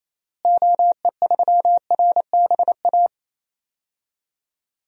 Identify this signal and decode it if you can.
Morse code, heard as a keyed tone, about 28 wpm, 700 Hz. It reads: OE3RBA
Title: DXCC-100 Call Signs - 6 Characters - Repeated 3x - 28wpm